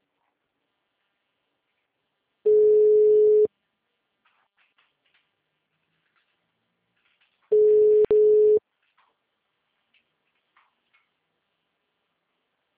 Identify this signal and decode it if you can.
Ringback tone (Europe, 425 Hz).